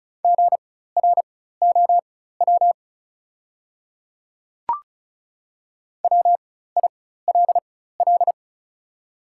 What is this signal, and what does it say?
Morse code, heard as a keyed tone, about 36 wpm, spaced as for 16 wpm, 700 Hz. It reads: GROW WILL
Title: 4x Character Spacing - Top 300 Words - 35wpm